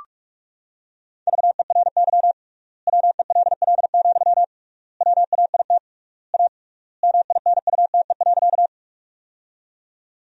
Morse code, heard as a keyed tone, about 46 wpm, 700 Hz. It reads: VERY WELL, WAIT A MINUTE.
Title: 3x Word-Spacing - Sentences from Top 500 Words - 45wpm